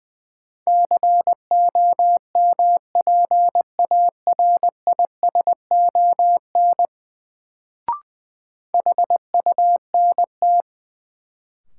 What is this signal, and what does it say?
Morse code, heard as a keyed tone, about 20 wpm, 700 Hz. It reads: COMPARISON HUNT
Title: Sentences from Top 1600 Words - New Word Drill - 20wpm